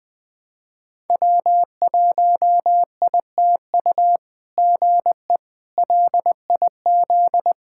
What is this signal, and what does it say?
Morse code, heard as a keyed tone, about 21 wpm, 700 Hz. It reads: W1ITU GE LIZ
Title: SOTA QSO Practice 20wpm